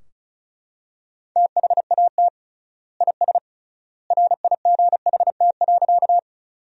Morse code, heard as a keyed tone, about 34 wpm, 700 Hz. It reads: THAT IS RIGHT.
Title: 3x Word-Spacing - Sentences from Top 200 Words - 35wpm